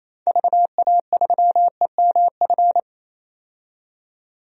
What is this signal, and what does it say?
Morse code, heard as a keyed tone, about 29 wpm, 700 Hz. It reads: VA3EMF
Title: DXCC-100 Call Signs - 6 Characters - 28wpm